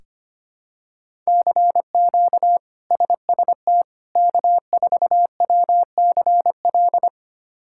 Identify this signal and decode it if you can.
Morse code, heard as a keyed tone, about 25 wpm, 700 Hz. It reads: CQ SST K4WCL
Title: K1USN SST QSO Practice - 25wpm